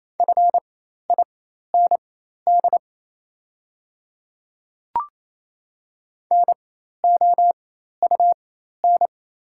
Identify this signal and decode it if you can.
Morse code, heard as a keyed tone, about 27 wpm, spaced as for 13 wpm, 700 Hz. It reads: FIND NOUN
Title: 4x Character Spacing - Top 500 Words - 28wpm